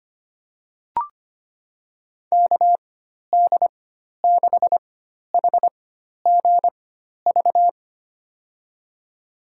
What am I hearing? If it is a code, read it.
Morse code, heard as a keyed tone, about 25 wpm, spaced as for 12 wpm, 700 Hz. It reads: KD6HGV
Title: USA Call Signs - 6 Characters - 4x Character Spacing - 25wpm